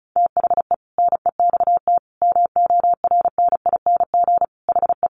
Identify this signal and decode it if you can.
Morse code, heard as a keyed tone, about 34 wpm, 700 Hz. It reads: THE NEXT MORNING HE
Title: Sentences from Top 600 Words 35wpm